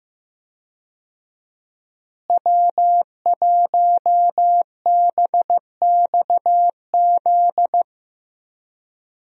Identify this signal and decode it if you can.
Morse code, heard as a keyed tone, about 16 wpm, 700 Hz. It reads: W1BXZ
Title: USA Call Signs - 5 Characters - Repeated 2x - 15wpm